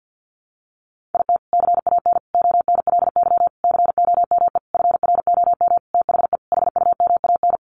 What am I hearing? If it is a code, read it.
Morse code, heard as a keyed tone, about 48 wpm, 700 Hz. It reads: IT CAN ONLY COME FROM THE HUMAN
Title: Sentences from Top 1400 Words 50wpm